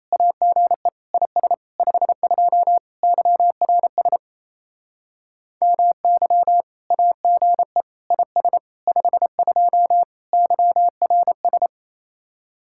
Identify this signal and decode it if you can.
Morse code, heard as a keyed tone, about 35 wpm, 700 Hz. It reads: AGE IS 52 YRS MY AGE IS 52 YRS
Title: Speed-Racing Common QSO Phrases Encore - 28wpm